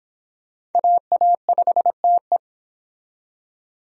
Morse code, heard as a keyed tone, about 30 wpm, 700 Hz. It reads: AA5TE